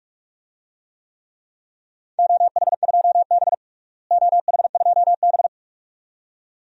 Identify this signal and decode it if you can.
Morse code, heard as a keyed tone, about 42 wpm, 700 Hz. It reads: OH2B OH2B
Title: DXCC-100 Call Signs - 4 Characters - Repeated 2x - 45wpm